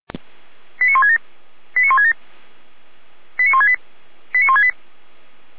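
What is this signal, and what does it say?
Five-tone selective call (ZVEI), heard as quick runs of five tones, tones of about 80 ms. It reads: 89167 89167 89168 89168
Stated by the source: Received by SDR; ZVEI five-tone selective call.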